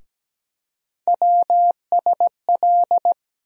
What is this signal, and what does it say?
Morse code, heard as a keyed tone, about 17 wpm, 700 Hz. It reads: WSL
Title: USA Call Sign Postfixes - 3 Letters 17wpm